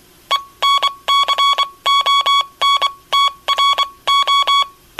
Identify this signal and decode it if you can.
Morse code, heard as a keyed tone, about 24 wpm, spaced as for 20 wpm, 1120 Hz. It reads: ENCONTRO
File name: Aniversario 25 anos Encontro Dx- Radio aparecida